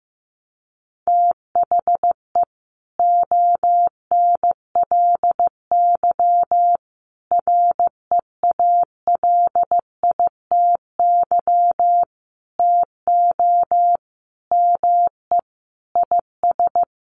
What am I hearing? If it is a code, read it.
Morse code, heard as a keyed tone, about 15 wpm, 700 Hz. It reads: THE ONLY REALITY TO ME IS
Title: Sentences from Top 1200 Words 15wpm